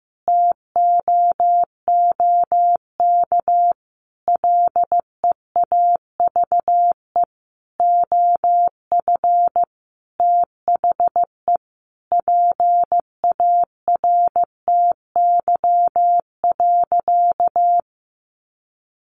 Morse code, heard as a keyed tone, about 15 wpm, 700 Hz. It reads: TOOK LEAVE OF THE PARTY.
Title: Sentences from Top 1400 Words 15wpm